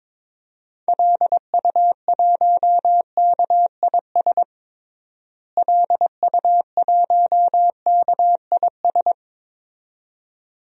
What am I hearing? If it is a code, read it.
Morse code, heard as a keyed tone, about 22 wpm, 700 Hz. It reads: LU1KIS LU1KIS
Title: DXCC-100 Call Signs - 6 Characters - Repeated 2x - 22wpm